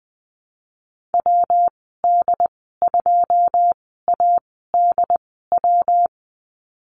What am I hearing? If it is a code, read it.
Morse code, heard as a keyed tone, about 20 wpm, spaced as for 15 wpm, 700 Hz. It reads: WD2ADW
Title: 128 Lesson 21g - Number 6 - Call Signs 20wpm